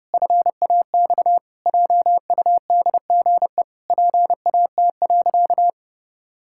Morse code, heard as a keyed tone, about 31 wpm, 700 Hz. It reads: FAX JUDGE PAT.
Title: Pangram Challenge v2 - 30wpm